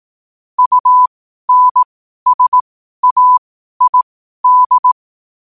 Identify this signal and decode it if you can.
Morse code, heard as a keyed tone, about 18 wpm, spaced as for 12 wpm, 1000 Hz. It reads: UNSAID
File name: qotdfiles / 12wpm